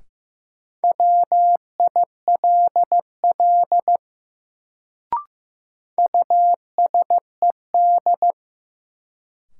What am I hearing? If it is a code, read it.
Morse code, heard as a keyed tone, about 15 wpm, 700 Hz. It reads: WILL USED
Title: Sets of 2 Words - New Word Drill - 15wpm